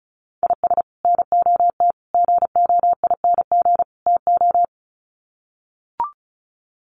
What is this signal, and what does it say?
Morse code, heard as a keyed tone, about 34 wpm, 700 Hz. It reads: IS NOT GOING TO
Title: Sets of 4 Words 35wpm